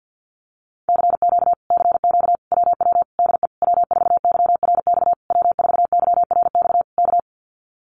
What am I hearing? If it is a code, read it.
Morse code, heard as a keyed tone, about 50 wpm, 700 Hz. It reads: CQ CQ WW DE W4YRX W4YRX K
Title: Common QSO Phrases Encore 50wpm